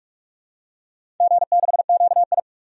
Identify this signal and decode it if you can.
Morse code, heard as a keyed tone, about 46 wpm, 700 Hz. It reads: G6QI
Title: DXCC-100 Call Signs - 4 Characters - 45wpm